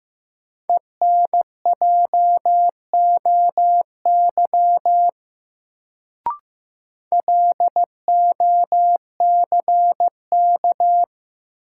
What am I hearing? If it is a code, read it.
Morse code, heard as a keyed tone, about 15 wpm, 700 Hz. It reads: ENJOY LOCK
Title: Sentences from Top 1300 Words - New Word Drill Alt - 15wpm